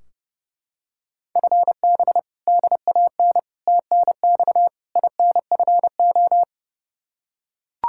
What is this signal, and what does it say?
Morse code, heard as a keyed tone, about 31 wpm, 700 Hz. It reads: FB DAN TNX INFO
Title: Practice QSO - Exchange 3 - More Info - 30wpm